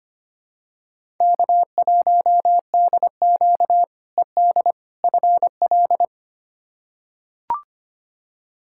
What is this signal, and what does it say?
Morse code, heard as a keyed tone, about 25 wpm, 700 Hz. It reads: K1DQ ED FL
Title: K1USN SST QSO Practice - 25wpm